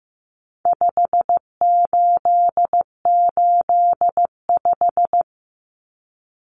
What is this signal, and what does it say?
Morse code, heard as a keyed tone, about 15 wpm, 700 Hz. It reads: 5885
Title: Numbers - 4-Digit 15wpm